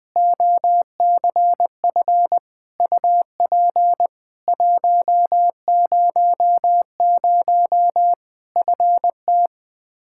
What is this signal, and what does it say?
Morse code, heard as a keyed tone, about 20 wpm, 700 Hz. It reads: OCF UP 100 FT
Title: Practice QSO - Exchange 2 - Info - 20wpm